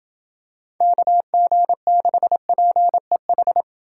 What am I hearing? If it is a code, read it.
Morse code, heard as a keyed tone, about 28 wpm, 700 Hz. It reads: KG6PEH